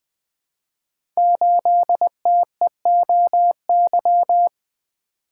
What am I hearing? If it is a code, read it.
Morse code, heard as a keyed tone, about 20 wpm, 700 Hz. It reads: 8TEOY
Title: Code Groups - 5 Letter:Numbers - 20wpm